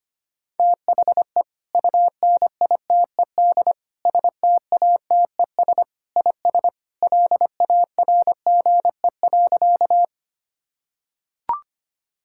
Morse code, heard as a keyed tone, about 25 wpm, 700 Hz. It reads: THE UNITED STATES IS LARGE.
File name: practice / 25wpm / Sentences with 3-Word Phrases - 25wpm